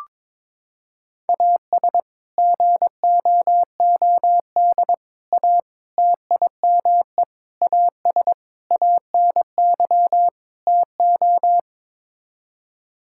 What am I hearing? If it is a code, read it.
Morse code, heard as a keyed tone, about 22 wpm, 700 Hz. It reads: AS GOOD A TIME AS ANY TO
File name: practice / 22wpm / Sets of 7 Words - 22wpm